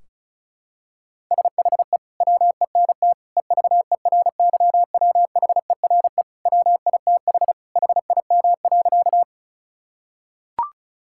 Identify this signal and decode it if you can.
Morse code, heard as a keyed tone, about 34 wpm, 700 Hz. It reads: SHE WENT EVERYWHERE WITH HIM.
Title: Sentences from Top 1600 Words 35wpm